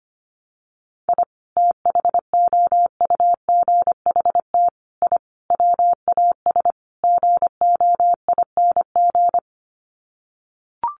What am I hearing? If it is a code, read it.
Morse code, heard as a keyed tone, about 25 wpm, 700 Hz. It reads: I THOUGHT I WAS GOING
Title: Sets of 5 Words Encore2 25wpm